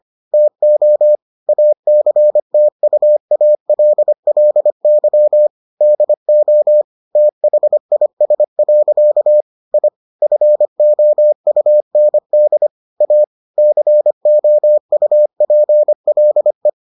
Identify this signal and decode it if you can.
Morse code, heard as a keyed tone, about 25 wpm, 600 Hz. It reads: TO ACTUALLY DO THIS. I FOUND A COUPLE